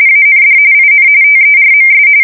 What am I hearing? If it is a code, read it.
RTTY, about 45.45 baud. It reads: AND IN TERMS
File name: fsk170 / usnavy4